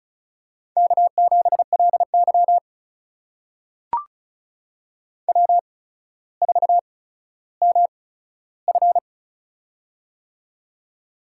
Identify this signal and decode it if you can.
Morse code, heard as a keyed tone, about 36 wpm, spaced as for 15 wpm, 700 Hz. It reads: K7LY W4MF
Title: USA Call Signs - 4 Characters - 8x Character Spacing - 35wpm